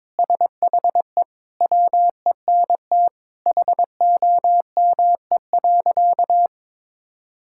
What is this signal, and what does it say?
Morse code, heard as a keyed tone, about 22 wpm, 700 Hz. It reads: SHE WENT HOME.
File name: practice / 22wpm / Sentences with 2-Word Phrases - 22wpm